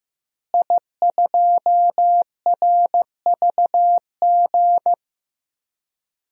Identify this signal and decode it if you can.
Morse code, heard as a keyed tone, about 15 wpm, 700 Hz. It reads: I2RVG
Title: DXCC-100 Call Signs - 5 Characters - Repeated 3x - 15wpm